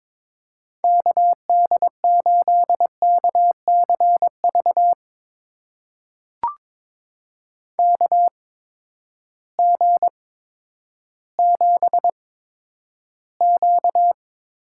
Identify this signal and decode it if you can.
Morse code, heard as a keyed tone, about 22 wpm, spaced as for 11 wpm, 700 Hz. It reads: KD8KCV KG7Q
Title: USA Call Signs - 6 Characters - 8x Character Spacing - 22wpm